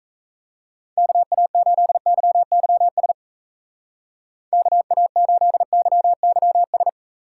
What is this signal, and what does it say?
Morse code, heard as a keyed tone, about 34 wpm, 700 Hz. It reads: KA8YYS KA8YYS
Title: Speed Racing - USA Call Signs 28wpm